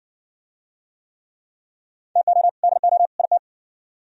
Morse code, heard as a keyed tone, about 57 wpm, 700 Hz. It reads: TO DO IT